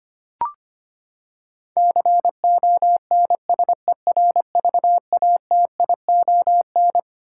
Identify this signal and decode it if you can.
Morse code, heard as a keyed tone, about 25 wpm, 700 Hz. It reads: CONSERVATION
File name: Twelve Letter Words - 25wpm